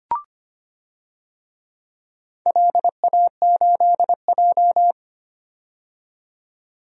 Morse code, heard as a keyed tone, about 25 wpm, 700 Hz. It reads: LA8J
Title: DXCC-100 Call Signs - 4 Characters - 25wpm